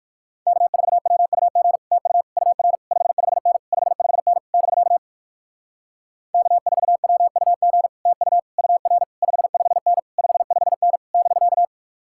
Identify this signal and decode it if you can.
Morse code, heard as a keyed tone, about 63 wpm, 700 Hz. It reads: K4WUG TU UR 55N 55N <BK> K4WUG TU UR 55N 55N <BK>